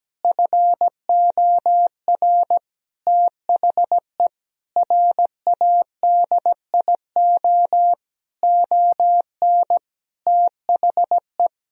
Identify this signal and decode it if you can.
Morse code, heard as a keyed tone, about 17 wpm, 700 Hz. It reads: FOR THE RADIO ON THE
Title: Sentences from Top 1000 Words - Review Part 3 - 17wpm